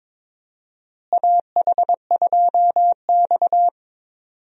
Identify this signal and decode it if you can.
Morse code, heard as a keyed tone, about 22 wpm, 700 Hz. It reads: AH2X
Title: USA Call Signs - 4 Characters - Repeated 2x - 22wpm